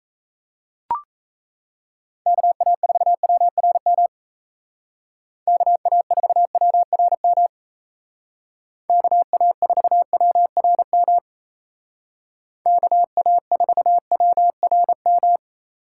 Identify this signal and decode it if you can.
Morse code, heard as a keyed tone, about 34 wpm, 700 Hz. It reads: KA4WRM KA4WRM KA4WRM KA4WRM
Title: Speed Racing - USA 2x3 Call Signs 28wpm